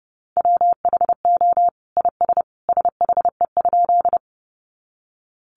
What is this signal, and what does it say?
Morse code, heard as a keyed tone, about 30 wpm, 700 Hz. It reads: WHO IS SHE?
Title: Sentences from Top 100 Words 30wpm